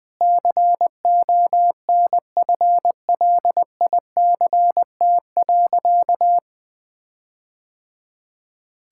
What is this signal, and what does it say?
Morse code, heard as a keyed tone, about 20 wpm, 700 Hz. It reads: CONFLICT.
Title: All-Age Appropriate Jokes - 20wpm